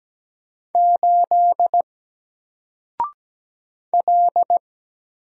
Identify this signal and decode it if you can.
Morse code, heard as a keyed tone, about 17 wpm, 700 Hz. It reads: 8 L
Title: Single Letter-Number 17wpm